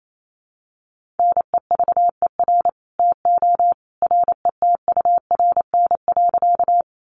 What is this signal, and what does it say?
Morse code, heard as a keyed tone, about 28 wpm, 700 Hz. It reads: NEVER TO RETURN.
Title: Sentences from Top 800 Words 28wpm